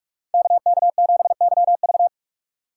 Morse code, heard as a keyed tone, about 42 wpm, 700 Hz. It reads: KK7YV